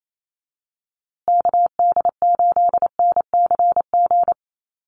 Morse code, heard as a keyed tone, about 28 wpm, 700 Hz. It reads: KD8NCG
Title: USA Call Signs Repeated 3x 28wpm